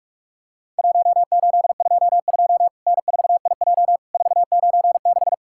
Morse code, heard as a keyed tone, about 46 wpm, 700 Hz. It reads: 1822 N4IJ 496
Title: Worked All Europe DX Contest - QTC Exchange - 45wpm